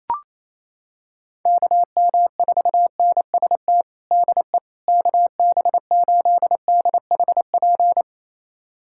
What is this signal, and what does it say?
Morse code, heard as a keyed tone, about 28 wpm, 700 Hz. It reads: KM4NST DE KB8DHP
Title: USA Callsign DE Callsign 28wpm